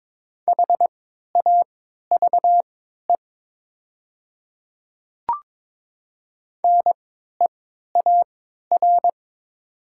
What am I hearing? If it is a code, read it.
Morse code, heard as a keyed tone, about 22 wpm, spaced as for 12 wpm, 700 Hz. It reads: HAVE NEAR
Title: 3x Character Spacing - Top 300 Words - 22wpm